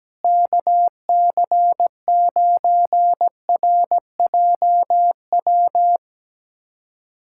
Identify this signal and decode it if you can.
Morse code, heard as a keyed tone, about 17 wpm, 700 Hz. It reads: KC9RJW
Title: USA Callsign DE Callsign - 6 Characters - 17wpm